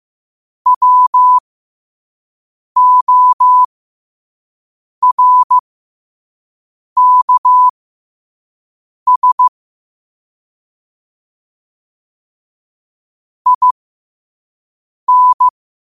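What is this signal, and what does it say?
Morse code, heard as a keyed tone, about 15 wpm, spaced as for 5 wpm, 1000 Hz. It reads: WORKS IN
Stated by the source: Morse code sent at 5 wpm for learning.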